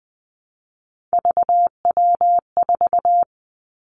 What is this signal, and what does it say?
Morse code, heard as a keyed tone, about 20 wpm, 700 Hz. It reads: VW4
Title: Three Challenging Letter-Number Combinations 20wpm